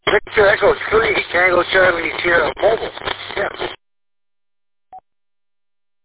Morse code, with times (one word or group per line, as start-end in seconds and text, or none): none